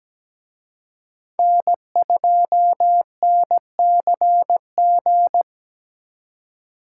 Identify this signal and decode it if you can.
Morse code, heard as a keyed tone, about 20 wpm, 700 Hz. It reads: N2NCG
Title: Speed Racing - USA 1x3 Call Signs 17wpm